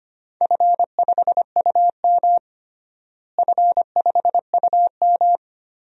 Morse code, heard as a keyed tone, about 25 wpm, 700 Hz. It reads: F5UM F5UM
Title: DXCC-100 Call Signs - 4 Characters - Repeated 2x - 25wpm